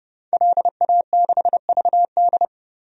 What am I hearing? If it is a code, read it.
Morse code, heard as a keyed tone, about 30 wpm, 700 Hz. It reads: LA6VD
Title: DXCC-100 Call Signs - 5 Characters - Repeated 3x - 30wpm